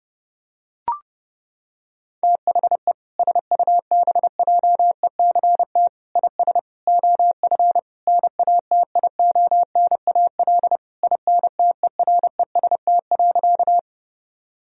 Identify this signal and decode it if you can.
Morse code, heard as a keyed tone, about 30 wpm, 700 Hz. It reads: THE SUBJECT IS OF NATIONAL INTEREST.
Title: Sentences from Top 1000 Words - Review Part 4 - 30wpm